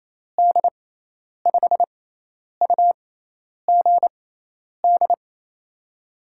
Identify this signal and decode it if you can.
Morse code, heard as a keyed tone, about 29 wpm, spaced as for 10 wpm, 700 Hz. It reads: D5UGD
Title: USA Call Signs - 6x Character Spacing - 28wpm